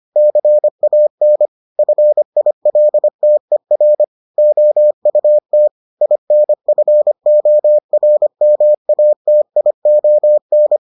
Morse code, heard as a keyed tone, about 25 wpm, 600 Hz. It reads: CAN FILTER OUT INFORMATION